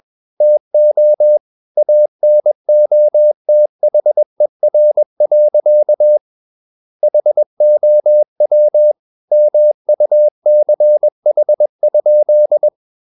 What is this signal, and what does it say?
Morse code, heard as a keyed tone, about 21 wpm, 600 Hz. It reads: TO ANOTHER. HOW MUCH?
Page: CW med Gnister